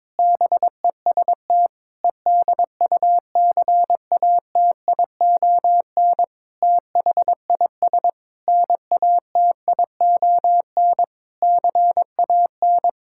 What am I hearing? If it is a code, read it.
Morse code, heard as a keyed tone, about 22 wpm, 700 Hz. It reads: BEST EDUCATION THIS NATION CAN
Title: Sentences from Top 1000 Words - Review Part 6 - 22wpm